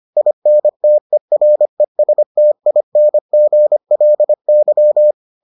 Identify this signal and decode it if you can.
Morse code, heard as a keyed tone, about 25 wpm, 600 Hz. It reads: INTERESTINGLY